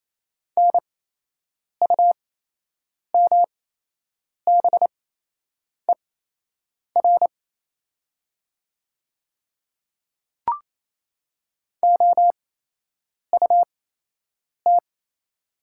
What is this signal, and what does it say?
Morse code, heard as a keyed tone, about 27 wpm, spaced as for 8 wpm, 700 Hz. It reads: NUMBER OUT
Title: 8x Character Spacing - Top 100 Words - 28wpm